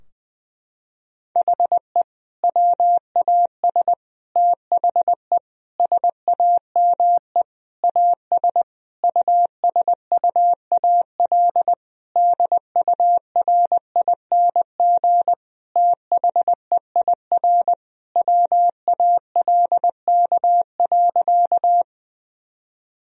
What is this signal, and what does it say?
Morse code, heard as a keyed tone, about 20 wpm, 700 Hz. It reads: HE WAS THE SAME AS USUAL DURING THEIR WALK.
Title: Sentences from Top 1000 Words - Review Part 2 - 20wpm